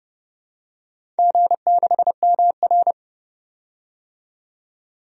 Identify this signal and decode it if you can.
Morse code, heard as a keyed tone, about 29 wpm, 700 Hz. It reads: G6MR
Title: DXCC-100 Call Signs - 4 Characters - 30wpm